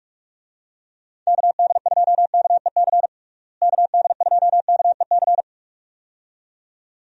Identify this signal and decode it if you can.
Morse code, heard as a keyed tone, about 44 wpm, 700 Hz. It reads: KD2KEC KD2KEC
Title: USA Call Signs - 6 Characters - Repeated 2x - 45wpm